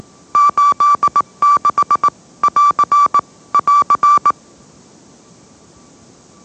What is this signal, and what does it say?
Morse code, heard as a keyed tone, about 18 wpm, 1160 Hz. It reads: 86 <AR> <AR>
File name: mrx / mrx081 20wpm